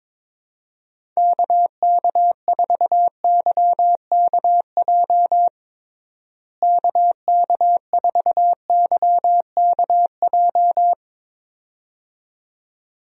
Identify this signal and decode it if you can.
Morse code, heard as a keyed tone, about 22 wpm, 700 Hz. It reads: KK4YKJ KK4YKJ
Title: USA Call Signs - 6 Characters - Repeated 2x - 22wpm